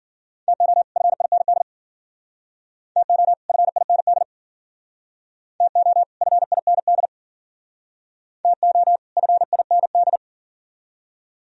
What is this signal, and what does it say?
Morse code, heard as a keyed tone, about 53 wpm, 700 Hz. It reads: TO FIND TO FIND TO FIND TO FIND